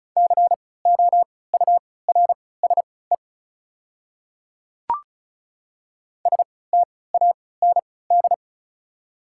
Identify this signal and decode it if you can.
Morse code, heard as a keyed tone, about 34 wpm, spaced as for 20 wpm, 700 Hz. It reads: COURSE STAND
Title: 3x Character Spacing - Top 500 Words - 35wpm